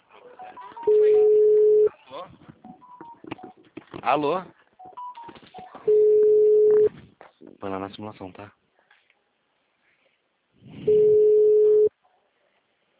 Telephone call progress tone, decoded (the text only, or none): ringback tone